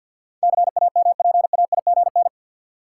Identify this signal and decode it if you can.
Morse code, heard as a keyed tone, about 47 wpm, 700 Hz. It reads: CAMPAIGN
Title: Sentences from Top 1200 Words - New Word Drill Alt - 50wpm